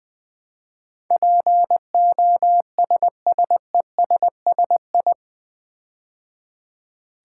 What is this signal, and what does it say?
Morse code, heard as a keyed tone, about 20 wpm, 700 Hz. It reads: POSSESSI
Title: Build A Word - 4 Letters At A Time - 10-Letter Words - 20wpm